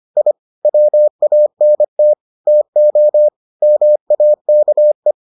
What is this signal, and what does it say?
Morse code, heard as a keyed tone, about 25 wpm, 600 Hz. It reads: I WANT TO MAKE